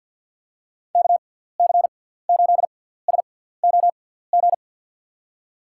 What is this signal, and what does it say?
Morse code, heard as a keyed tone, about 50 wpm, spaced as for 17 wpm, 700 Hz. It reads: KC8SOG
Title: USA Call Signs - 6 Characters - 6x Character Spacing - 50wpm